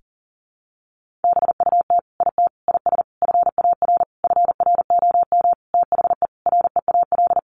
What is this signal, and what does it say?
Morse code, heard as a keyed tone, about 40 wpm, 700 Hz. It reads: BUT IT IS FAR FROM THE REAL
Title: Sentences from Top 1400 Words 40wpm